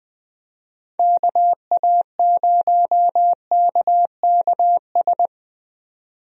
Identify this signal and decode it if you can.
Morse code, heard as a keyed tone, about 20 wpm, 700 Hz. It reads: KA0KKS
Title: USA Call Signs - 6 Characters - 20wpm